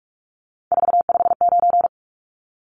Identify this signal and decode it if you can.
Morse code, heard as a keyed tone, about 44 wpm, 700 Hz. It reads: 459
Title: Numbers - 3-Digit 45wpm